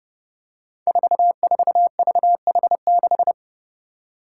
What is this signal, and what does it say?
Morse code, heard as a keyed tone, about 30 wpm, 700 Hz. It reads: 44VH6
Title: Five Challenging Letter-Number Combinations 30wpm